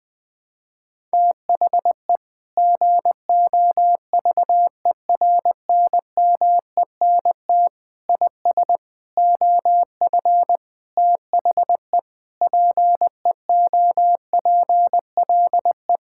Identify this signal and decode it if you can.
Morse code, heard as a keyed tone, about 20 wpm, 700 Hz. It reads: THE GOVERNMENT IS OF THE PEOPLE
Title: Sentences with 9-Word Phrases - 20wpm